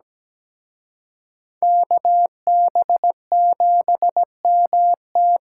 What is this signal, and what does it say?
Morse code, heard as a keyed tone, about 20 wpm, 700 Hz. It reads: KB7MT
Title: Speed Racing - USA Call Signs 17wpm